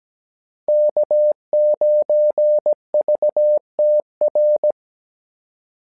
Morse code, heard as a keyed tone, about 17 wpm, 600 Hz. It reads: K9VTR
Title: SOTA QSO Practice 17wpm